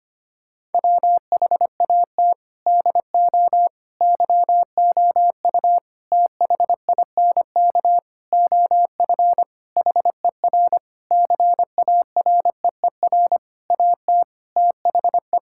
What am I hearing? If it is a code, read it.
Morse code, heard as a keyed tone, about 25 wpm, 700 Hz. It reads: WHAT DO YOU THINK OF HER CAREER AT THE
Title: Sentences from Top 1000 Words - Review Part 6 - 25wpm